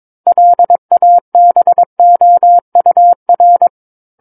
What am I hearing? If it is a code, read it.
Morse code, heard as a keyed tone, about 22 wpm, 700 Hz. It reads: LABOUR
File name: Book One / 13-Chapter 13 - 22 WPM